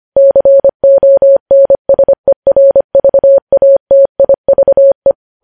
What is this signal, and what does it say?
Morse code, heard as a keyed tone, about 25 wpm, 550 Hz. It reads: CONSERVATIVE
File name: podcast / News headlines-1x-25x25